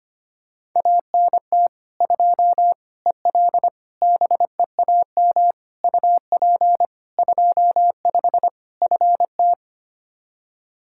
Morse code, heard as a keyed tone, about 25 wpm, 700 Hz. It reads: ANT 2 EL BEAM UP 25 FT
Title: Practice QSO - Exchange 2 - Info - 25wpm